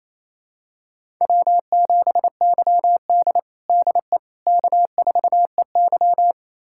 Morse code, heard as a keyed tone, about 28 wpm, 700 Hz. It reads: W7YD DE K4EY
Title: USA Callsign DE Callsign - 4 Characters - 28wpm